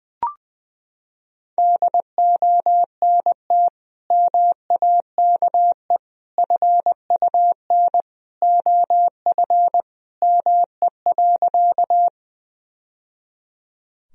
Morse code, heard as a keyed tone, about 21 wpm, 700 Hz. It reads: DONT MAKE FUN OF ME.
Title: Sentences from Top 1000 Words - Review Part 4 - 20wpm